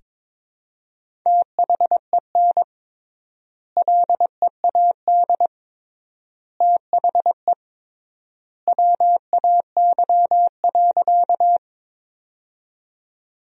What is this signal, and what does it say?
Morse code, heard as a keyed tone, about 22 wpm, 700 Hz. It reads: THEN LEAD THE WAY.
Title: 3x Word-Spacing - Sentences from Top 500 Words - 22wpm